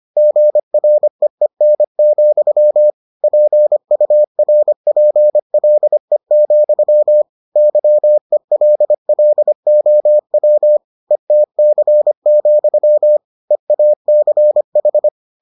Morse code, heard as a keyed tone, about 25 wpm, 600 Hz. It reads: GREEN, PURPLE, YELLOW ETC, EACH